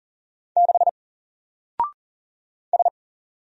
Morse code, heard as a keyed tone, about 40 wpm, 700 Hz. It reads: 6 S
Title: Single Character (Letter:Number:Punctuation:Prosign) 40wpm